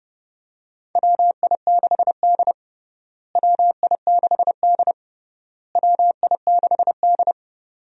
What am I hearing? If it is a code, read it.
Morse code, heard as a keyed tone, about 32 wpm, 700 Hz. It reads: WI6D WI6D WI6D
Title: USA Call Signs - 4 Characters - Repeated 3x - 30wpm